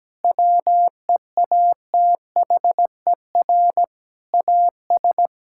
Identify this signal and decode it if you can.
Morse code, heard as a keyed tone, about 17 wpm, 700 Hz. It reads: WEATHER AS
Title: Sentences from Top 1000 Words - Review Part 5 - 17wpm